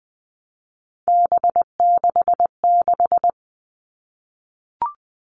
Morse code, heard as a keyed tone, about 20 wpm, 700 Hz. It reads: B66
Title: All Cut Numbers - 3-Digit 20wpm